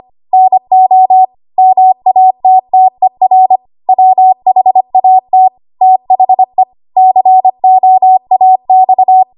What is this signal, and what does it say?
Morse code, heard as a keyed tone, about 25 wpm, 750 Hz. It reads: NO MATTER WHAT THE COAX
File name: Archive / 25 WPM / 25WPM